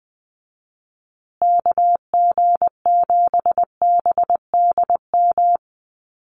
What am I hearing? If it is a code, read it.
Morse code, heard as a keyed tone, about 20 wpm, 700 Hz. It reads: KG7BDM
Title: USA Call Signs Repeated 2x 20wpm